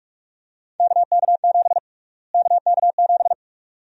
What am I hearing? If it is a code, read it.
Morse code, heard as a keyed tone, about 46 wpm, 700 Hz. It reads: KK7 KK7
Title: USA Call Sign Prefixes - Repeated 2x - 45wpm